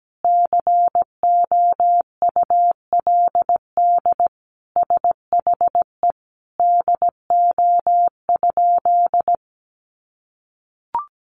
Morse code, heard as a keyed tone, about 17 wpm, 700 Hz. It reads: COULD SHE DO?
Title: Sentences from Top 800 Words 17wpm